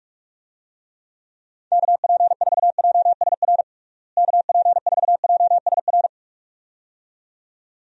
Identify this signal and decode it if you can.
Morse code, heard as a keyed tone, about 48 wpm, 700 Hz. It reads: KP4JSR KP4JSR
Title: USA Call Signs - 6 Characters - Repeated 2x - 45wpm